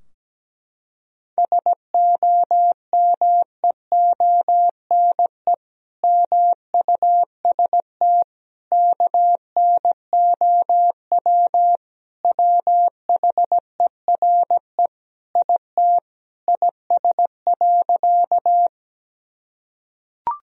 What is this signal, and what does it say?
Morse code, heard as a keyed tone, about 17 wpm, 700 Hz. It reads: SOMEONE MUST KNOW WHERE IT IS.
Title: Sentences from Top 1000 Words - Review Part 4 - 17wpm